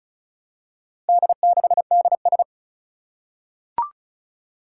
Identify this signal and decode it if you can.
Morse code, heard as a keyed tone, about 36 wpm, 700 Hz. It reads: D6DS
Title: Four Challenging Letter-Number Combinations 35wpm